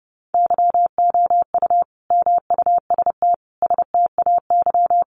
Morse code, heard as a keyed tone, about 30 wpm, 700 Hz. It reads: YOU MUST STAY
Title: Sentences from Top 500 Words 30wpm